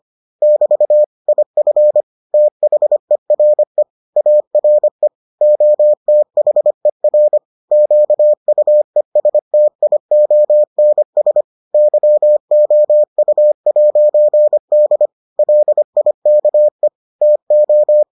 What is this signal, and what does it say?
Morse code, heard as a keyed tone, about 25 wpm, 600 Hz. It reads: <BT> IF THERE ARE OTHER QUESTIONS YOU'D LIKE TO